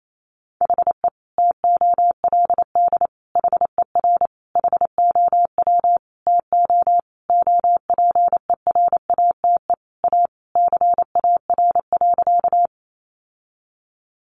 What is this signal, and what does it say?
Morse code, heard as a keyed tone, about 28 wpm, 700 Hz. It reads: HE TOLD HER HOW TO OPERATE A CAR.
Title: Sentences from Top 1500 Words 28wpm